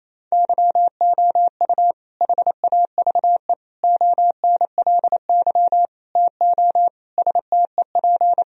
Morse code, heard as a keyed tone, about 28 wpm, 700 Hz. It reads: YOU HAVE ONLY TO STEP
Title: Sentences from Top 1000 Words - Review Part 4 - 28wpm